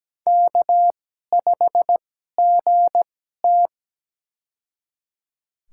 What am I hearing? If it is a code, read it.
Morse code, heard as a keyed tone, about 17 wpm, spaced as for 12 wpm, 700 Hz. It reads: K5GT
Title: USA Call Signs - 4 Characters - 2x Character Spacing - 17wpm